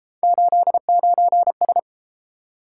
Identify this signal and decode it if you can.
Morse code, heard as a keyed tone, about 29 wpm, 700 Hz. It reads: 89S